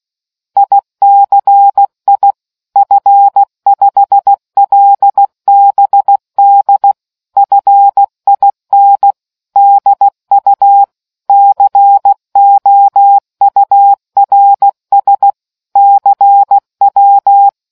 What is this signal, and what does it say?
Morse code, heard as a keyed tone, about 16 wpm, 800 Hz. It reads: ICI F5LBD FIN DU COURS CW